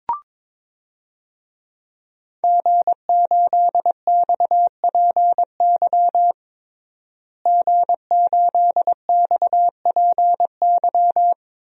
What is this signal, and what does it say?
Morse code, heard as a keyed tone, about 22 wpm, 700 Hz. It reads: G8XPY G8XPY
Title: DXCC-100 Call Signs - 5 Characters - Repeated 3x - 22wpm